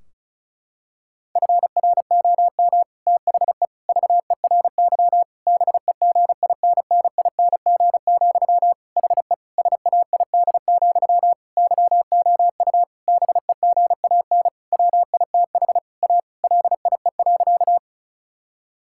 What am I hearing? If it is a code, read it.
Morse code, heard as a keyed tone, about 34 wpm, 700 Hz. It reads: FROM THE VERY BEGINNING, HE SAID, YOU BEGAN WITH A LIE.
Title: Sentences from Top 1600 Words 35wpm